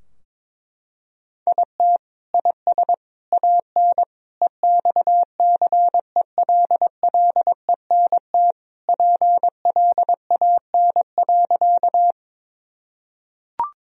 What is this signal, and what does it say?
Morse code, heard as a keyed tone, about 22 wpm, 700 Hz. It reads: IT IS AN EXCELLENT PLAN.
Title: Sentences from Top 1000 Words - Review Part 2 - 22wpm